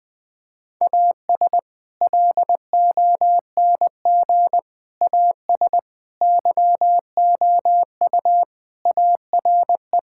Morse code, heard as a keyed tone, about 21 wpm, 700 Hz. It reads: AS LONG AS YOU ARE
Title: Sentences with 4-Word Phrases - 20wpm